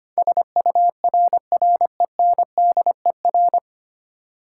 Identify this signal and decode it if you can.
Morse code, heard as a keyed tone, about 25 wpm, 700 Hz. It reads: SURRENDER
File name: Nine Letter Words - 25wpm